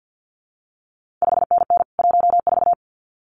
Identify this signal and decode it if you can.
Morse code, heard as a keyed tone, about 50 wpm, 700 Hz. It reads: 5NN 14
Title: CQ WW DX Contest Exchange - 50wpm